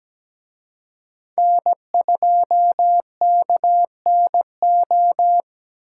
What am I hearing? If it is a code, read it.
Morse code, heard as a keyed tone, about 20 wpm, 700 Hz. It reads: N2KNO
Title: Speed Racing - USA 1x3 Call Signs 17wpm